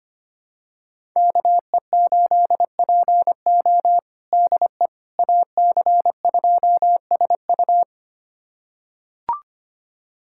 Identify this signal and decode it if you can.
Morse code, heard as a keyed tone, about 25 wpm, 700 Hz. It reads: KE8PO DE AC2SU
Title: USA Callsign DE Callsign - 5 Characters - 25wpm